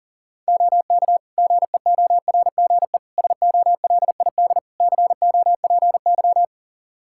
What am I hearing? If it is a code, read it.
Morse code, heard as a keyed tone, about 40 wpm, 700 Hz. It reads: OK GEORGE SOLID COPY
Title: K1USN SST QSO Practice 40wpm